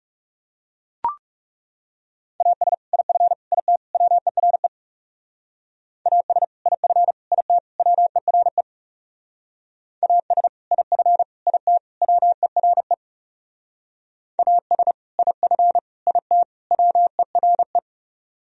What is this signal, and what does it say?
Morse code, heard as a keyed tone, about 38 wpm, 700 Hz. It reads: AS IF IT WERE AS IF IT WERE AS IF IT WERE AS IF IT WERE